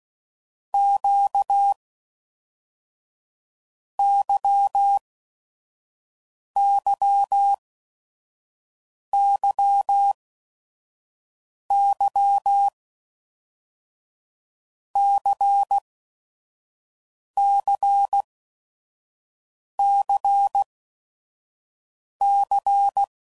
Morse code, heard as a keyed tone, about 16 wpm, 800 Hz. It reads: Q Y Y Y Y C C C C
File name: Basic Letter Groups By7Groups / LFCYQ / LFCYQ@5 f15